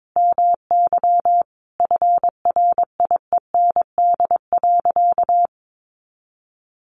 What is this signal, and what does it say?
Morse code, heard as a keyed tone, about 22 wpm, 700 Hz. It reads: MY FRIEND.
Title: Sentences from Top 1300 Words 22wpm